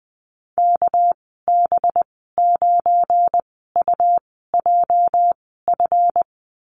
Morse code, heard as a keyed tone, about 20 wpm, spaced as for 14 wpm, 700 Hz. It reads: KB9UJF
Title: 207 Lesson 33g - Letter j - Call Signs 20wpm